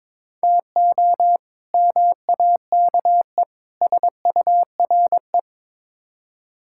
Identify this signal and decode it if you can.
Morse code, heard as a keyed tone, about 22 wpm, 700 Hz. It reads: TO MAKE SURE
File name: practice / 22wpm / Build A Phrase - 5-Word Phrases - 22wpm